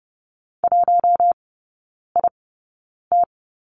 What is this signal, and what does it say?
Morse code, heard as a keyed tone, about 30 wpm, spaced as for 9 wpm, 700 Hz. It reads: 1IT
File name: Lesson 17e - Letter u - Three Characters 30wpm